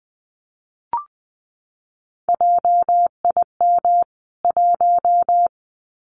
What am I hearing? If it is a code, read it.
Morse code, heard as a keyed tone, about 20 wpm, 700 Hz. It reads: JIM 1
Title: CWT Exchange 20wpm